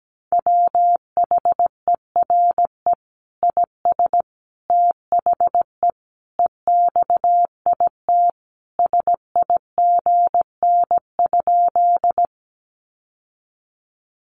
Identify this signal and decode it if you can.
Morse code, heard as a keyed tone, about 17 wpm, 700 Hz. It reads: WHERE IS THE EXIT SIGN?
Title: Sentences from Top 1100 Words 17wpm